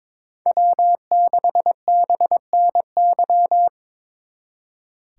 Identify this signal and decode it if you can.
Morse code, heard as a keyed tone, about 25 wpm, 700 Hz. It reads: W6BNY
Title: Speed Racing - USA Call Signs 22wpm